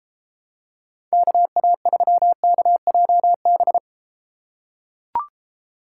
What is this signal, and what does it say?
Morse code, heard as a keyed tone, about 31 wpm, 700 Hz. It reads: KA3KJB